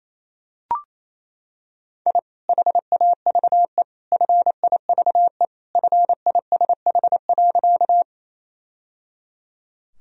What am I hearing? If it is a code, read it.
Morse code, heard as a keyed tone, about 28 wpm, 700 Hz. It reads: I HAVE FIVE FISH.
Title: Sentences from Top 1000 Words - Review Part 2 - 28wpm